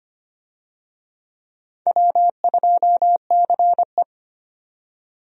Morse code, heard as a keyed tone, about 25 wpm, 700 Hz. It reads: W2CE
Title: USA Call Signs - 4 Characters - 25wpm